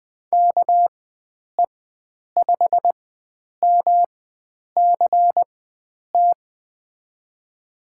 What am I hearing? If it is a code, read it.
Morse code, heard as a keyed tone, about 21 wpm, spaced as for 9 wpm, 700 Hz. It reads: KE5MCT
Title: USA Call Signs - 6 Characters - 4x Character Spacing - 20wpm